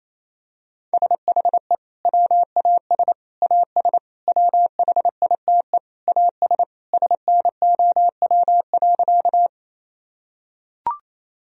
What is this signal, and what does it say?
Morse code, heard as a keyed tone, about 28 wpm, 700 Hz. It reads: SHE WAS AS WHITE AS SNOW.
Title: Sentences from Top 1000 Words - Review Part 3 - 28wpm